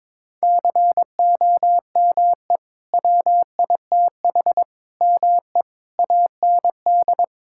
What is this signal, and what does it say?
Morse code, heard as a keyed tone, about 22 wpm, 700 Hz. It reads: COME WITH ME AND
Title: Sentences from Top 1600 Words 22wpm